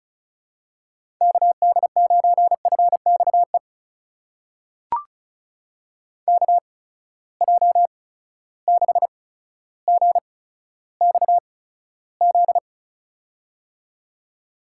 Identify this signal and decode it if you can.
Morse code, heard as a keyed tone, about 36 wpm, spaced as for 15 wpm, 700 Hz. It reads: KD9FXE KJ6GXZ
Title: USA Call Signs - 6 Characters - 8x Character Spacing - 35wpm